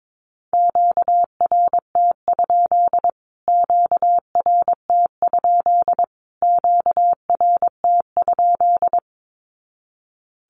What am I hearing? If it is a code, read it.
Morse code, heard as a keyed tone, about 22 wpm, 700 Hz. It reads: QRT? QRT? QRT?
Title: Q-Codes Repeated 3x 22wpm